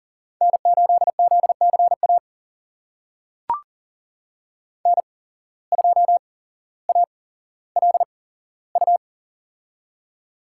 Morse code, heard as a keyed tone, about 40 wpm, spaced as for 17 wpm, 700 Hz. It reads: N8ZCA N2ALU
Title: USA Call Signs - 5 Characters - 8x Character Spacing - 40wpm